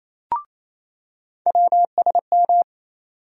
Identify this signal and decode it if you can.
Morse code, heard as a keyed tone, about 28 wpm, 700 Hz. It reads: WSM
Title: USA Call Sign Postfixes - 3 Letters 28wpm